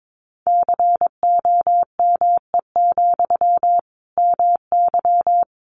Morse code, heard as a keyed tone, about 22 wpm, 700 Hz. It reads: COME, MY
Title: Sentences from Top 800 Words 22wpm